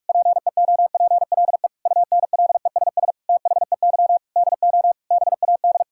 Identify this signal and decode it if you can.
Morse code, heard as a keyed tone, about 41 wpm, 700 Hz. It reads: PEOPLE UNLESS THEY DO BAD
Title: Sentences from Top 1000 Words - Review Part 6 - 45wpm